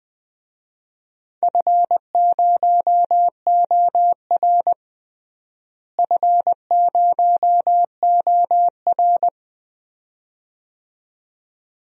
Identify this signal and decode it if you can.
Morse code, heard as a keyed tone, about 20 wpm, 700 Hz. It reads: F0OR F0OR
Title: DXCC-100 Call Signs - 4 Characters - Repeated 2x - 20wpm